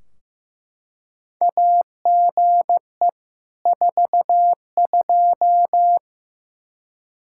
Morse code, heard as a keyed tone, about 15 wpm, 700 Hz. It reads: AGE 42
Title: Practice QSO - Exchange 3 - More Info - 15wpm